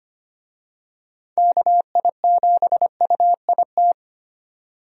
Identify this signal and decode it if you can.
Morse code, heard as a keyed tone, about 25 wpm, 700 Hz. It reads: KI7UIT
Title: USA Call Signs - 6 Characters - 25wpm